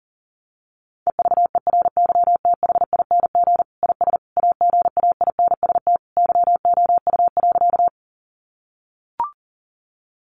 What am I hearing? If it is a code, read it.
Morse code, heard as a keyed tone, about 37 wpm, 700 Hz. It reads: EVERYTHING IS AGAINST YOU.
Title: Sentences from Top 700 Words 40wpm